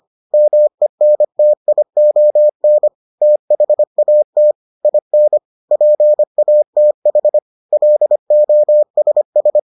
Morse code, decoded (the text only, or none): MENTION THAT IN PATH LOSS